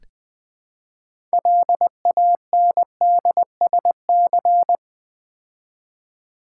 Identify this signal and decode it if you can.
Morse code, heard as a keyed tone, about 20 wpm, 700 Hz. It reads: LANDSC
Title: Build A Word - 3 Letters At A Time - 9-Letter Words - 20wpm